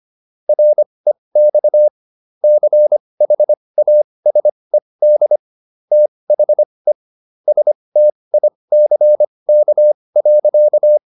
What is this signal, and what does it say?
Morse code, heard as a keyed tone, about 25 wpm, spaced as for 20 wpm, 600 Hz. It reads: REX CHASED THE STICK.